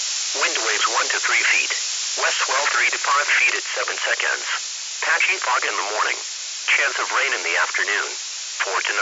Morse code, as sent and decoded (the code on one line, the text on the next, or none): none